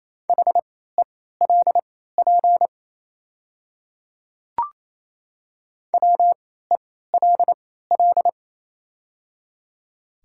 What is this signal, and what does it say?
Morse code, heard as a keyed tone, about 28 wpm, spaced as for 16 wpm, 690 Hz. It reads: HELP WELL
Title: 3x Character Spacing - Top 200 Words - 28wpm